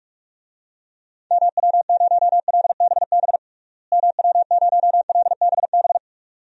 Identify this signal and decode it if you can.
Morse code, heard as a keyed tone, about 42 wpm, 700 Hz. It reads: MW0LBB MW0LBB
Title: DXCC-100 Call Signs - 6 Characters - Repeated 2x - 45wpm